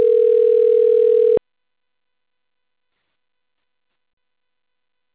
Ringback tone (North America), 440 + 480 Hz.
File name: rg-1600-unknown- - - .7221